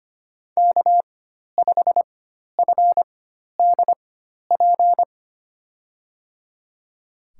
Morse code, heard as a keyed tone, about 25 wpm, spaced as for 12 wpm, 700 Hz. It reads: K5FDP
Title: USA Call Signs - 5 Characters - 4x Character Spacing - 25wpm